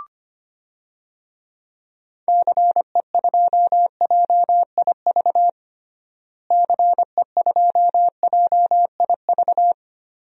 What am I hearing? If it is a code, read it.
Morse code, heard as a keyed tone, about 25 wpm, 700 Hz. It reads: CE2JIV CE2JIV
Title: DXCC-100 Call Signs - 6 Characters - Repeated 3x - 25wpm